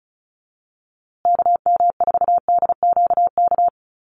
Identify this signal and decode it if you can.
Morse code, heard as a keyed tone, about 34 wpm, 700 Hz. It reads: KM4DQK
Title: USA Call Signs Repeated 2x 35wpm